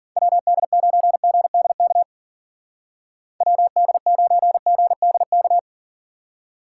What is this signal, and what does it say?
Morse code, heard as a keyed tone, about 48 wpm, 700 Hz. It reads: WD9GDK WD9GDK
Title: Speed Racing - USA 2x3 Call Signs 40wpm